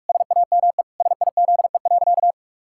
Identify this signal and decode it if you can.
Morse code, heard as a keyed tone, about 46 wpm, 700 Hz. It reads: SAME SIZE.
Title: Pangram Challenge v2 - 45wpm